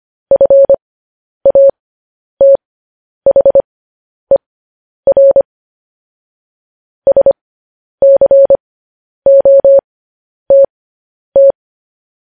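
Morse code, heard as a keyed tone, about 25 wpm, spaced as for 10 wpm, 550 Hz. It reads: FATHER SCOTT
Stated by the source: Morse code news headlines broadcast, opening with vvv.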